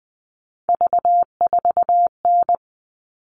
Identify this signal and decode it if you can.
Morse code, heard as a keyed tone, about 20 wpm, 700 Hz. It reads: V4N
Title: All Cut Numbers - 3-Digit 20wpm